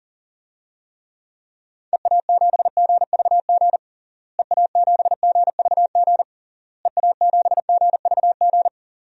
Morse code, heard as a keyed tone, about 40 wpm, 700 Hz. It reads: EA7GVG EA7GVG EA7GVG
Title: DXCC-100 Call Signs - 6 Characters - Repeated 3x - 40wpm